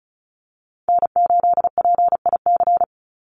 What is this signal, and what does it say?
Morse code, heard as a keyed tone, about 34 wpm, 700 Hz. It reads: N8PIC
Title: USA Call Signs 35wpm